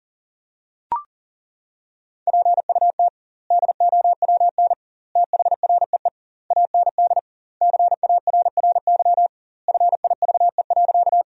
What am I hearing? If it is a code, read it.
Morse code, heard as a keyed tone, about 40 wpm, 700 Hz. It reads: PUT DOWN THREE AND CARRY FIVE.
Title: 2x Word-Spacing - Sentences from Top 400 Words - 40wpm